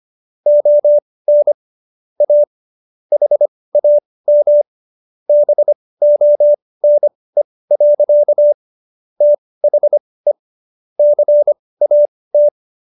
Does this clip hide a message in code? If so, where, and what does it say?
Yes: Morse code from 0.5 to 12.5 s, heard as a keyed tone, about 25 wpm, spaced as for 18 wpm, 590 Hz. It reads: ON A HAM BONE. THE CAT